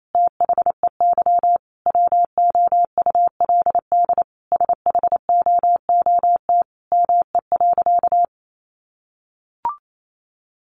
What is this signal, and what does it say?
Morse code, heard as a keyed tone, about 28 wpm, 700 Hz. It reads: THEY WOULD SHOOT ME.
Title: Sentences from Top 1500 Words 28wpm